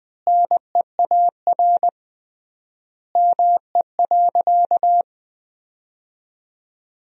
Morse code, heard as a keyed tone, about 20 wpm, 700 Hz. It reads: NEAR ME.
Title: 3x Word-Spacing - Sentences from Top 300 Words - 20wpm